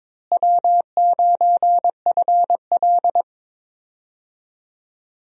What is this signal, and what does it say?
Morse code, heard as a keyed tone, about 21 wpm, 700 Hz. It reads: W9FL
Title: USA Call Signs - 4 Characters - Repeated 2x - 22wpm